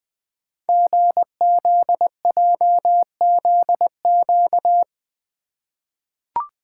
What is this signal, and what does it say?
Morse code, heard as a keyed tone, about 20 wpm, 700 Hz. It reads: GZJZQ
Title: Code Groups - 5 Letter:Numbers - 20wpm